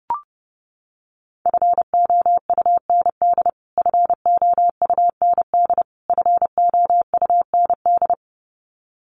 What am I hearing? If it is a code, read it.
Morse code, heard as a keyed tone, about 30 wpm, 700 Hz. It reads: FOUND FOUND FOUND
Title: Top 300 Words Repeated 30wpm